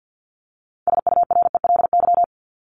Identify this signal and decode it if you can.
Morse code, heard as a keyed tone, about 48 wpm, 700 Hz. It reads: SURELY
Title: LY Suffix - Top 100 Words with Base Words 50wpm